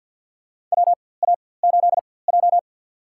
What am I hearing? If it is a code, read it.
Morse code, heard as a keyed tone, about 53 wpm, spaced as for 24 wpm, 700 Hz. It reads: WA8J
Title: USA Call Signs - 4x Character Spacing - 50wpm